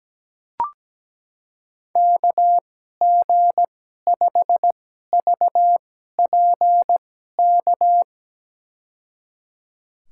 Morse code, heard as a keyed tone, about 17 wpm, spaced as for 12 wpm, 700 Hz. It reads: KG5VPK
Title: USA Call Signs - 2x Character Spacing - 17wpm